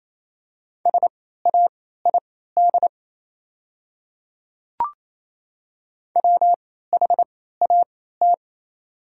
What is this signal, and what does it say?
Morse code, heard as a keyed tone, about 28 wpm, spaced as for 16 wpm, 700 Hz. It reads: SAID WHAT
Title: 3x Character Spacing - Top 100 Words - 28wpm